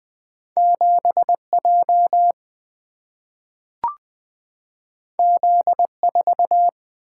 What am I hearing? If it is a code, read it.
Morse code, heard as a keyed tone, about 20 wpm, 700 Hz. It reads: 7 J Z 4
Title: Single-Letter Single-Number 20wpm